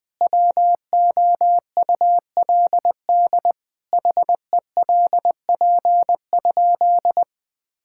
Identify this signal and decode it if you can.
Morse code, heard as a keyed tone, about 20 wpm, 700 Hz. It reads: WOULD HELP?
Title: Sentences with 9-Word Phrases - 20wpm